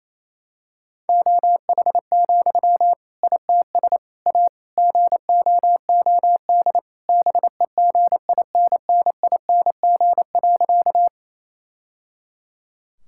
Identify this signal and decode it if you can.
Morse code, heard as a keyed tone, about 28 wpm, 700 Hz. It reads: OH, ITS A GOOD BEGINNING.
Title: Sentences from Top 1000 Words - Review Part 5 - 28wpm